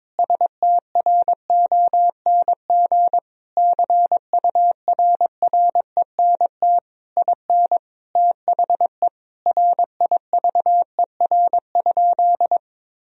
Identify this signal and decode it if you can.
Morse code, heard as a keyed tone, about 21 wpm, 700 Hz. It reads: STRONG CURRENT IN THE RIVER?
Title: Sentences from Top 1000 Words - Review Part 4 - 22wpm